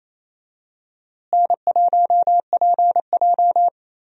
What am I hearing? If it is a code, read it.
Morse code, heard as a keyed tone, about 28 wpm, 700 Hz. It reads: N1PJ
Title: USA Call Signs - 4 Characters - Repeated 2x - 28wpm